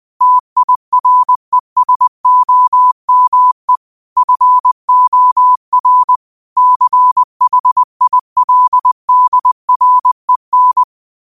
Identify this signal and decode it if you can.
Morse code, heard as a keyed tone, about 20 wpm, 1000 Hz. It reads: TIRESOME FOR CHILDREN
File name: qotdfiles / 20wpm